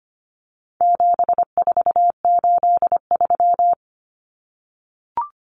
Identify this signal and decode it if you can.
Morse code, heard as a keyed tone, about 25 wpm, 700 Hz. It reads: 7483
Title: Common Cut Numbers - 4-Digit 25wpm